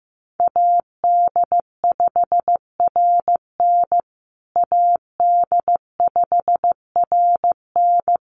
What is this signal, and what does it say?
Morse code, heard as a keyed tone, about 15 wpm, 700 Hz. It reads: AD5RN AD5RN
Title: USA Call Signs Repeated 2x 15wpm